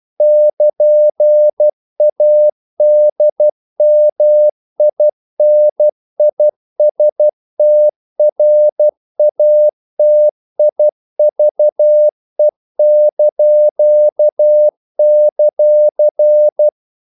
Morse code, heard as a keyed tone, about 12 wpm, 600 Hz. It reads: <KN>ADMINISTRATIVE);
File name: CW / Practice / NationalCapitals12